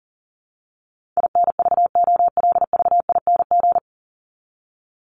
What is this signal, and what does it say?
Morse code, heard as a keyed tone, about 40 wpm, 700 Hz. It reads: INVOLVING
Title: ING Suffix - Top 100 Words with Base Words 40wpm